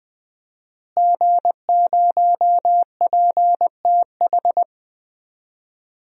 Morse code, heard as a keyed tone, about 20 wpm, 700 Hz. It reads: G0PTH
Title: DXCC-100 Call Signs - 5 Characters - Repeated 2x - 20wpm